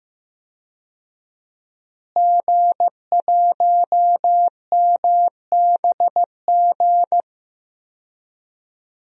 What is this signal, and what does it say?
Morse code, heard as a keyed tone, about 15 wpm, 700 Hz. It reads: G1MBG
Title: DXCC-100 Call Signs - 5 Characters - Repeated 3x - 15wpm